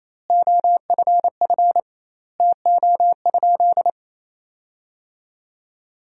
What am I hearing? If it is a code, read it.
Morse code, heard as a keyed tone, about 28 wpm, 700 Hz. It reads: OFF TO?
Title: 2x Word-Spacing - Sentences from Top 300 Words - 28wpm